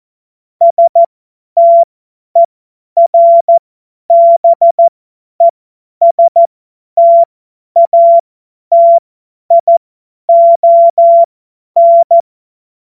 Morse code, heard as a keyed tone, about 14 wpm, spaced as for 10 wpm, 680 Hz. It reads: STERBESTATION
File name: Morsekurs / Witze / Witz 11. Juli 2023 50 F70